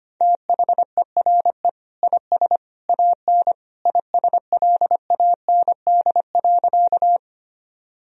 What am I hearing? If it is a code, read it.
Morse code, heard as a keyed tone, about 25 wpm, 700 Hz. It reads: THERE IS AN ISLAND.
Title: Sentences from Top 1000 Words - Review Part 3 - 25wpm